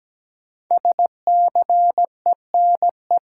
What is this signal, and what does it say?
Morse code, heard as a keyed tone, about 17 wpm, 700 Hz. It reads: SCENE
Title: Sentences from Top 1100 Words - New Word Drill - 17wpm